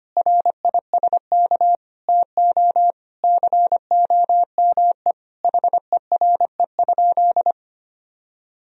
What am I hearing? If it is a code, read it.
Morse code, heard as a keyed tone, about 25 wpm, 700 Hz. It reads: RISK TO COME HERE?
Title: Sentences from Top 1000 Words - Review Part 4 - 25wpm